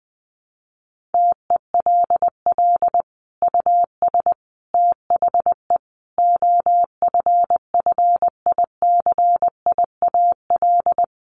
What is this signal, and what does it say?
Morse code, heard as a keyed tone, about 20 wpm, 700 Hz. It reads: TELL US THE OFFICIAL
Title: Sentences from Top 1400 Words 20wpm